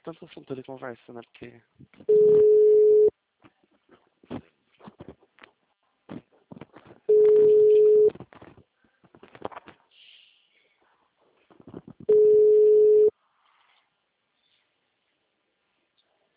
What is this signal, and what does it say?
Ringback tone (Europe, 425 Hz), with speech.
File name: out- -2013- - - .80303